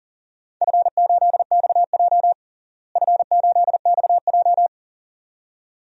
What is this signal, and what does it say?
Morse code, heard as a keyed tone, about 40 wpm, 700 Hz. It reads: F8XJ F8XJ
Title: DXCC-100 Call Signs - 4 Characters - Repeated 2x - 40wpm